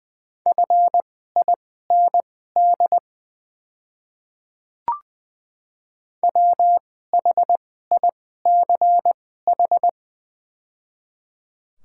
Morse code, heard as a keyed tone, about 20 wpm, spaced as for 14 wpm, 700 Hz. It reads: FIND WHICH
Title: 2x Character Spacing - Top 100 Words - 20wpm